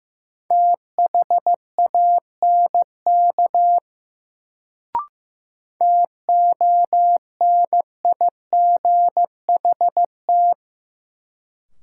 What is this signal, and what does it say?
Morse code, heard as a keyed tone, about 15 wpm, 700 Hz. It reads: THANK TONIGHT
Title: Sentences from Top 600 Words - New Word Drill - 15wpm